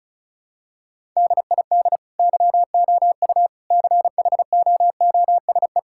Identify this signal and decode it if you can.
Morse code, heard as a keyed tone, about 36 wpm, 700 Hz. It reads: DID YOU CHOOSE
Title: Sentences from Top 1000 Words - Review Part 6 - 35wpm